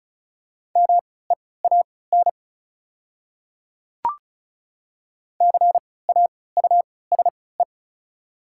Morse code, heard as a keyed tone, about 36 wpm, spaced as for 20 wpm, 700 Hz. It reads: MEAN CAUSE
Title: 3x Character Spacing - Top 300 Words - 35wpm